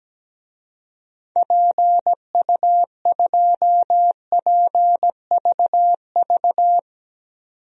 Morse code, heard as a keyed tone, about 17 wpm, 700 Hz. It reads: PU2PVV
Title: DXCC-100 Call Signs - 6 Characters - 17wpm